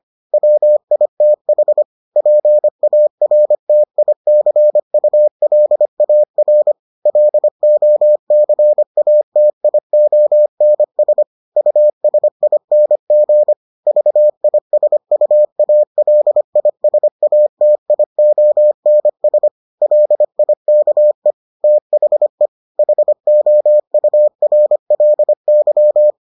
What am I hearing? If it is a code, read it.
Morse code, heard as a keyed tone, about 25 wpm, 600 Hz. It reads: WITH PARTICULAR LOCATIONS USING VISUALISATIONS LIKE THE HOURLY